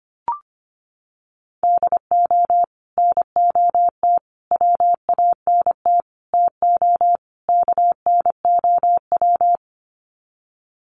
Morse code, heard as a keyed tone, about 25 wpm, 700 Hz. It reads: DO NOT WANT TO KNOW
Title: Sets of 5 Words Encore 25wpm